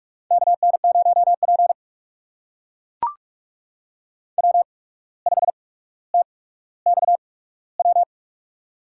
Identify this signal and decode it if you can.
Morse code, heard as a keyed tone, about 44 wpm, spaced as for 18 wpm, 700 Hz. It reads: KN0P W5TXW
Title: USA Call Signs - 8x Character Spacing - 45wpm